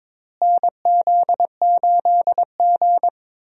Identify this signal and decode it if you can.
Morse code, heard as a keyed tone, about 21 wpm, 700 Hz. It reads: NZ8G
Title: USA Call Signs - 4 Characters - Repeated 2x - 22wpm